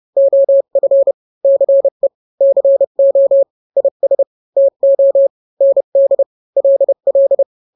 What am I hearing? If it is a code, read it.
Morse code, heard as a keyed tone, about 30 wpm, 550 Hz. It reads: OF CE CO IS TO ND LL
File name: n-grams / 2-grams 30 25 WPM